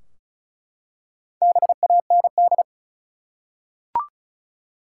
Morse code, heard as a keyed tone, about 36 wpm, 700 Hz. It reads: BAND
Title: Four Letter Words - 35wpm